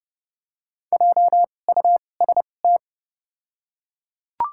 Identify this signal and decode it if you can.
Morse code, heard as a keyed tone, about 31 wpm, spaced as for 22 wpm, 700 Hz. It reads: JUST
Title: 2x Character Spacing - Top 300 Words - 30wpm